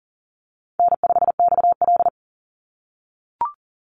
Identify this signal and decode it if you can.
Morse code, heard as a keyed tone, about 39 wpm, 700 Hz. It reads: N5XL
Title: NA SOTA Chaser Callsigns 40wpm